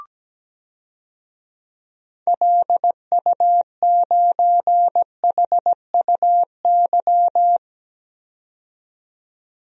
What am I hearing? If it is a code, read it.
Morse code, heard as a keyed tone, about 17 wpm, 700 Hz. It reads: LU9HUY
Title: DXCC-100 Call Signs - 6 Characters - 17wpm